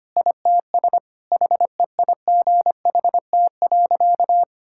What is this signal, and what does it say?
Morse code, heard as a keyed tone, about 25 wpm, 700 Hz. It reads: ITS HEIGHT.
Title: Sentences from Top 1600 Words 25wpm